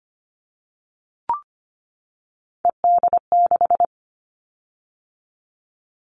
Morse code, heard as a keyed tone, about 25 wpm, 700 Hz. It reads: ED6
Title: Callsign Prefixes for Top 100 DXCC - 25wpm